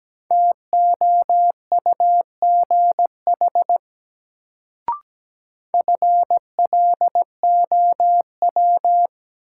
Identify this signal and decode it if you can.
Morse code, heard as a keyed tone, about 17 wpm, 700 Hz. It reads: TOUGH FLOW
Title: Sentences from Top 1200 Words - New Word Drill - 17wpm